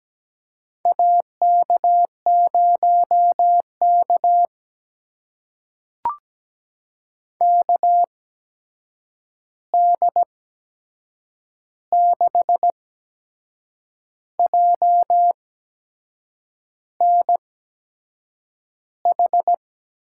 Morse code, heard as a keyed tone, about 17 wpm, spaced as for 6 wpm, 700 Hz. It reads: AK0K KD6JNH
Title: USA Call Signs - 8x Character Spacing - 17wpm